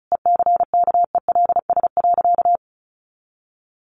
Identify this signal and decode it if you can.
Morse code, heard as a keyed tone, about 34 wpm, 700 Hz. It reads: ECKELS.
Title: A Sound Of Thunder 35wpm